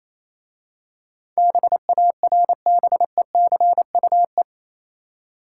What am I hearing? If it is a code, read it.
Morse code, heard as a keyed tone, about 27 wpm, 700 Hz. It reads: BARBECUE
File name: Easy Words - Speed-Racing with Hints - 22wpm